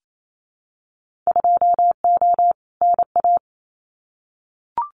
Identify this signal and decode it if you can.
Morse code, heard as a keyed tone, about 28 wpm, 700 Hz. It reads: 2O NA
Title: Winter Field Day Exchanges 28wpm